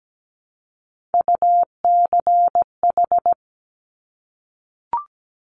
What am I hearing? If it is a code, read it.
Morse code, heard as a keyed tone, about 17 wpm, 690 Hz. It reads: UCH
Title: Top Three Letter Combinations in Words 17wpm